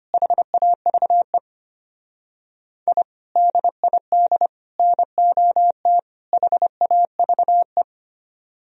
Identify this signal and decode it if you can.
Morse code, heard as a keyed tone, about 29 wpm, 700 Hz. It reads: HAVE I DID NOT HAVE
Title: Speed-Racing - Sets of 4 Words 25wpm